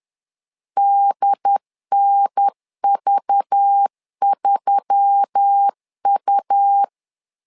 Morse code, heard as a keyed tone, about 10 wpm, 800 Hz. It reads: DNV3U
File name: LEZIONI / 5 LBVC / 5 LBV 10wpm 2248